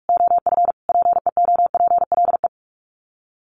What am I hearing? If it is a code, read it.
Morse code, heard as a keyed tone, about 44 wpm, 700 Hz. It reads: OF PEOPLE
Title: Sets of 5 Words Encore2 45wpm